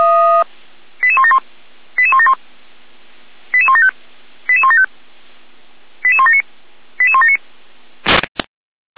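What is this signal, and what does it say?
Five-tone selective call (ZVEI), heard as quick runs of five tones, tones of about 70 ms. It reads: 88171 88171 88175 88175 88179 88179
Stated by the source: ZVEI five-tone call 88507 received via SDR, Feuerwehr Orlamünde.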